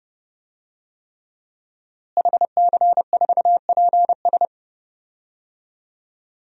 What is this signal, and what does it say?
Morse code, heard as a keyed tone, about 30 wpm, 700 Hz. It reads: HC4PS
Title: DXCC-100 Call Signs - 5 Characters - Repeated 3x - 30wpm